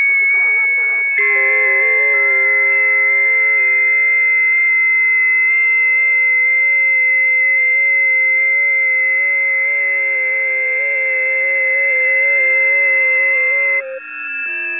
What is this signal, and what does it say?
FT8.
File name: 20m busy